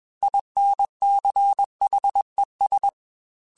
Morse code, heard as a keyed tone, about 21 wpm, 800 Hz. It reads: INCHES
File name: Weather Sets / Weather Sets-mp3s / Weather 3 17 @20